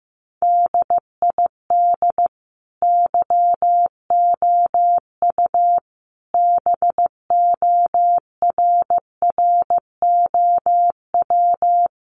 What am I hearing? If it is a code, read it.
Morse code, heard as a keyed tone, about 15 wpm, 700 Hz. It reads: DID YOU BORROW
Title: Sentences from Top 1500 Words 15wpm